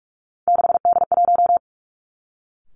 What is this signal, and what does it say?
Morse code, heard as a keyed tone, about 46 wpm, 700 Hz. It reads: 6D1
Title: Three Challenging Letter-Number Combinations 45wpm